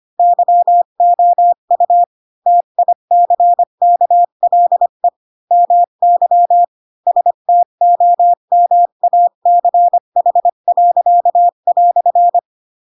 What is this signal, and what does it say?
Morse code, heard as a keyed tone, about 25 wpm, 700 Hz. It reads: YOU TICKLE MY STOMACH."
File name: Pinocchio / Pinocchio ch1 20 25 ik1hge